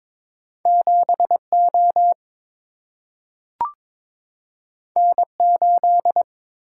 Morse code, heard as a keyed tone, about 22 wpm, 700 Hz. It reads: 7O N8
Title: Single-Letter Single-Number 22wpm